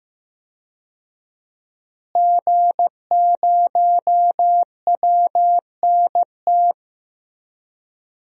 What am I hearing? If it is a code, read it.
Morse code, heard as a keyed tone, about 15 wpm, 700 Hz. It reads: G0WNT
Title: DXCC-100 Call Signs - 5 Characters - Repeated 2x - 15wpm